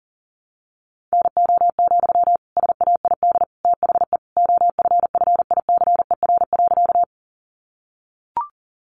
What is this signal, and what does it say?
Morse code, heard as a keyed tone, about 40 wpm, 700 Hz. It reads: NO, SAID THE OFFICER.
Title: Sentences from Top 1200 Words 40wpm